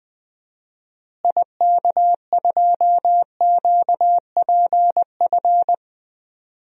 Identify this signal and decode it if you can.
Morse code, heard as a keyed tone, about 20 wpm, 700 Hz. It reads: IK2QPF
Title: DXCC-100 Call Signs - 6 Characters - Repeated 2x - 20wpm